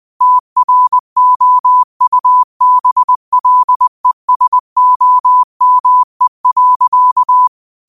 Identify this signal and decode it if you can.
Morse code, heard as a keyed tone, about 20 wpm, 1000 Hz. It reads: TROUBLESOME.
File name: qotdfiles / 20wpm